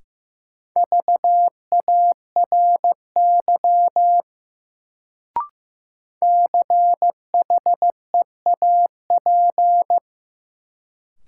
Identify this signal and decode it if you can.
Morse code, heard as a keyed tone, about 15 wpm, 700 Hz. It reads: VARY CHEAP
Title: Sentences from Top 1200 Words - New Word Drill Alt - 15wpm